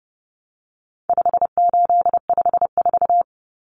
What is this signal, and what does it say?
Morse code, heard as a keyed tone, about 30 wpm, 700 Hz. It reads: 5854
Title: Common Cut Numbers - 4-Digit 30wpm